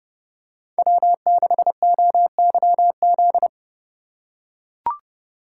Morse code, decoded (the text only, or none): W6OYZ